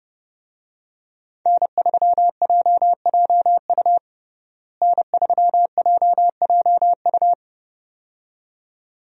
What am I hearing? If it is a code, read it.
Morse code, heard as a keyed tone, about 30 wpm, 700 Hz. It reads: N3JJU N3JJU
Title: USA Call Signs - 5 Characters - Repeated 2x - 30wpm